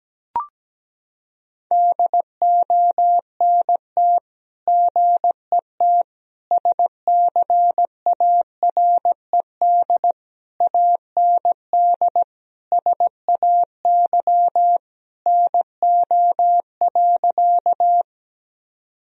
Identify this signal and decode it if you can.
Morse code, heard as a keyed tone, about 17 wpm, 700 Hz. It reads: DONT GET SCARED AND SAY NO.
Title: Sentences from Top 1000 Words - Review Part 4 - 17wpm